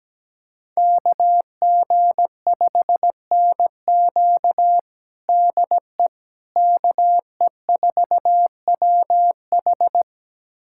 Morse code, heard as a keyed tone, about 17 wpm, 700 Hz. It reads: KG5NQ DE KE4WH
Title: USA Callsign DE Callsign 17wpm